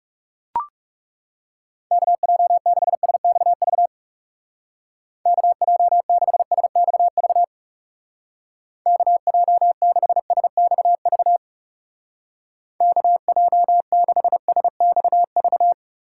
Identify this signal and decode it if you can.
Morse code, heard as a keyed tone, about 37 wpm, 700 Hz. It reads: KJ6SXV KJ6SXV KJ6SXV KJ6SXV
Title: Speed Racing - USA Call Signs 30wpm